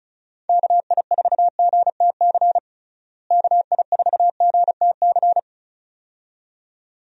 Morse code, heard as a keyed tone, about 36 wpm, 700 Hz. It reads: KI4GTC KI4GTC
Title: USA Call Signs - 6 Characters - Repeated 2x - 35wpm